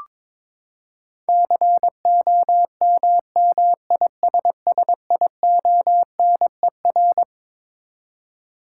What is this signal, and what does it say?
Morse code, heard as a keyed tone, about 22 wpm, 700 Hz. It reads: COMMISSIONER
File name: Twelve Letter Words - 22wpm